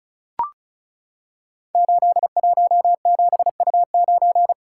Morse code, heard as a keyed tone, about 36 wpm, 690 Hz. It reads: 817U9
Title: Code Groups - 5 Letter:Numbers - 35wpm